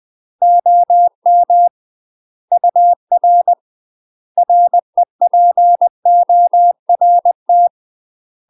Morse code, heard as a keyed tone, about 20 wpm, 700 Hz. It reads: OM UR REPORT
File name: UK MASTER QSOs TRAINING / 20WPM / UK MASTER QSOs 0032